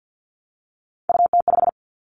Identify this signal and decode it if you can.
Morse code, heard as a keyed tone, about 50 wpm, 700 Hz. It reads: UT5
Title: Callsign Prefixes for Top 25 DXCC - 50wpm